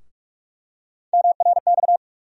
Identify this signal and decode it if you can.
Morse code, heard as a keyed tone, about 42 wpm, 700 Hz. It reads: MRX
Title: USA Call Sign Postfixes - 3 Letters 45wpm